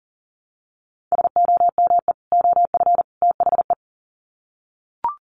Morse code, heard as a keyed tone, about 40 wpm, 700 Hz. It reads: SOME OF THE
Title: Sets of 3 Words 40wpm